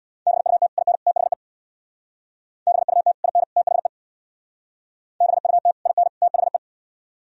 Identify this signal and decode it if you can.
Morse code, heard as a keyed tone, about 60 wpm, 700 Hz. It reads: BUT IN THE BUT IN THE BUT IN THE
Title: Speed-Racing - Sets of 3 Words 50wpm